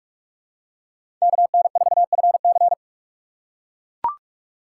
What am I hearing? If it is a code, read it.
Morse code, heard as a keyed tone, about 41 wpm, 700 Hz. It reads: KN4FC
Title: USA Call Signs - 4x Character Spacing - 45wpm